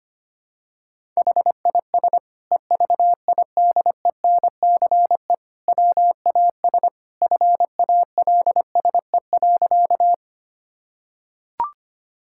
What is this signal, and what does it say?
Morse code, heard as a keyed tone, about 25 wpm, 700 Hz. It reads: HIS EVIDENCE WAS FALSE.
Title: Sentences from Top 1600 Words 25wpm